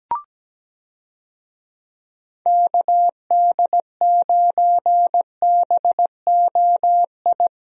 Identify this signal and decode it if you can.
Morse code, heard as a keyed tone, about 17 wpm, 700 Hz. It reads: KD9BOI
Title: USA Call Signs - 6 Characters - 17wpm